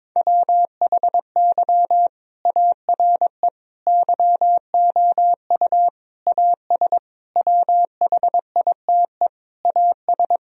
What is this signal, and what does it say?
Morse code, heard as a keyed tone, about 22 wpm, 700 Hz. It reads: WHY ARE YOU AS WHITE AS
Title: Sentences from Top 1000 Words - Review Part 3 - 22wpm